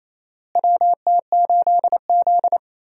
Morse code, heard as a keyed tone, about 28 wpm, 700 Hz. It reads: WT8Z
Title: USA Call Signs - 4 Characters - Repeated 3x - 28wpm